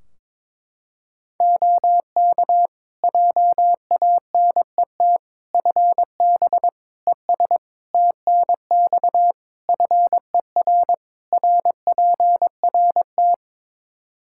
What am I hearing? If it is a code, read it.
Morse code, heard as a keyed tone, about 22 wpm, 700 Hz. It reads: OK JANET FB ES TNX FER RPRT
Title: Practice QSO - Exchange 2 - Info - 22wpm